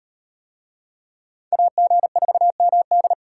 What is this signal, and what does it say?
Morse code, heard as a keyed tone, about 38 wpm, 700 Hz. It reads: AG4MD